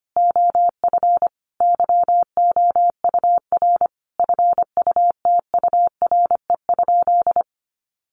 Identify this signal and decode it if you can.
Morse code, heard as a keyed tone, about 25 wpm, 700 Hz. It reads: OF YOUR FUTURE?
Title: Sentences from Top 700 Words 25wpm